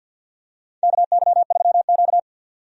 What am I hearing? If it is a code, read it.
Morse code, heard as a keyed tone, about 48 wpm, 700 Hz. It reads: KY3Q
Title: USA Call Signs - 4 Characters - 50wpm